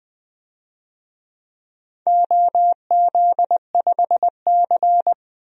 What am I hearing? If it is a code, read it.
Morse code, heard as a keyed tone, about 20 wpm, 700 Hz. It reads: OZ5C
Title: DXCC-100 Call Signs - 4 Characters - Repeated 2x - 20wpm